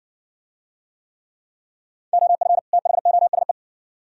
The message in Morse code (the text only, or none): OF THOSE